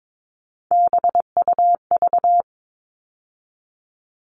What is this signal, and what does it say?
Morse code, heard as a keyed tone, about 22 wpm, 700 Hz. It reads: BUV
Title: All Cut Numbers - 3-Digit 22wpm